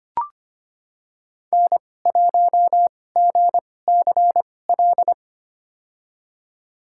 Morse code, heard as a keyed tone, about 25 wpm, spaced as for 18 wpm, 700 Hz. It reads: N1GCL
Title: USA Call Signs - 5 Characters - 2x Character Spacing - 25wpm